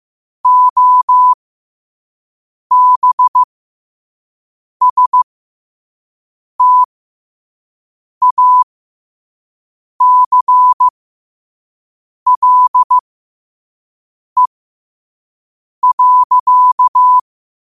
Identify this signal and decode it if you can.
Morse code, heard as a keyed tone, about 15 wpm, spaced as for 5 wpm, 1000 Hz. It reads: OBSTACLE.
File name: qotdfiles / 5wpm